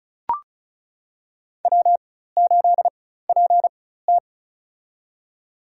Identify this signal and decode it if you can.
Morse code, heard as a keyed tone, about 36 wpm, spaced as for 16 wpm, 700 Hz. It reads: W8PT
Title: USA Call Signs - 4 Characters - 4x Character Spacing - 35wpm